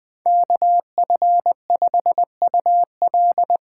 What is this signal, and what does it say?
Morse code, heard as a keyed tone, about 20 wpm, 700 Hz. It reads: KF5UL
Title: USA Call Signs - 6 Characters - 8x Character Spacing - 20wpm